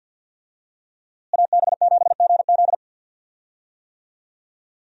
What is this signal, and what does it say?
Morse code, heard as a keyed tone, about 53 wpm, 700 Hz. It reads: AB7GZ
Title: USA Call Signs - 5 Characters - Repeated 2x - 50wpm